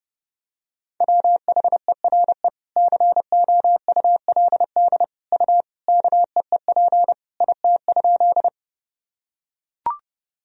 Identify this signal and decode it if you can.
Morse code, heard as a keyed tone, about 31 wpm, 700 Hz. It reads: WHERE COULD U KEEP IT?
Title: Sentences from Top 300 Words and Abbreviations 30wpm